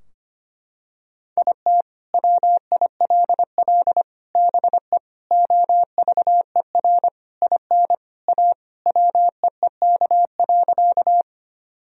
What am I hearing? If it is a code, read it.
Morse code, heard as a keyed tone, about 25 wpm, 700 Hz. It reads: IT WILL BE OVER IN A WEEK.
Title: Sentences from Top 1000 Words - Review Part 2 - 25wpm